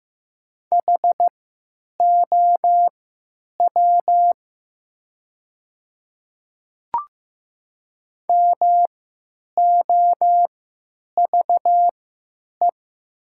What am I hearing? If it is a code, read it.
Morse code, heard as a keyed tone, about 15 wpm, spaced as for 9 wpm, 700 Hz. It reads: HOW MOVE
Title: 3x Character Spacing - Top 200 Words - 15wpm